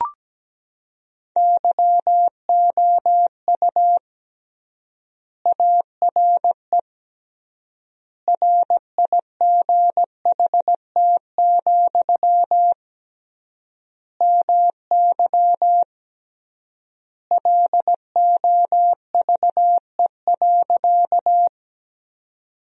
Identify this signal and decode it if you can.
Morse code, heard as a keyed tone, about 17 wpm, 700 Hz. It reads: YOU ARE RIGHT, MY LOVE.
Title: 3x Word-Spacing - Sentences from Top 400 Words - 17wpm